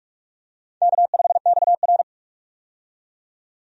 Morse code, heard as a keyed tone, about 42 wpm, 700 Hz. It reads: K5XR
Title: USA Call Signs - 4 Characters - 45wpm